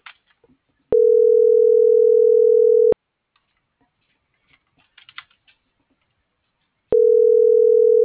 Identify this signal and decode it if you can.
Ringback tone (North America), 440 + 480 Hz.